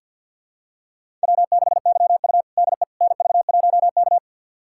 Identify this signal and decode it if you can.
Morse code, heard as a keyed tone, about 52 wpm, 700 Hz. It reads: W6YU DE NV1K
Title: USA Callsign DE Callsign - 4 Characters - 50wpm